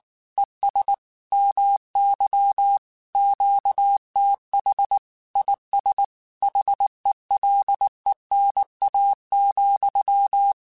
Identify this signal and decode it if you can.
Morse code, heard as a keyed tone, about 19 wpm, 800 Hz. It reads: ES MY QTH IS HELENA,
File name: QSOs / QSOs-mp3s / WD0BGP 1@18